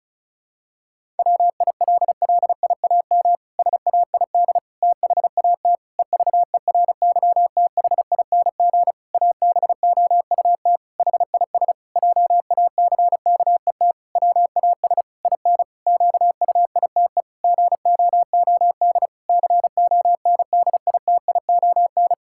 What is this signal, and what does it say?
Morse code, heard as a keyed tone, about 34 wpm, 700 Hz. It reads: WILLIAM SAID THAT EVERYTHING ABOUT HIS JACKET WAS IN QUITE GOOD CONDITION
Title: Pangram Challenge v2 - 35wpm